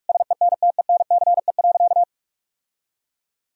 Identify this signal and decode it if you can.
Morse code, heard as a keyed tone, about 43 wpm, 700 Hz. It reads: SENTENCE.
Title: Sentences from Top 1000 Words - Review Part 1 - 45wpm